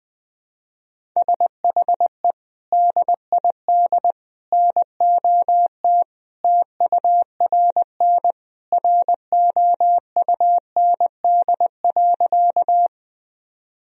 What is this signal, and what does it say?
Morse code, heard as a keyed tone, about 20 wpm, 700 Hz. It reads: SHE DID NOT TURN ROUND.
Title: Sentences from Top 1000 Words - Review Part 1 - 20wpm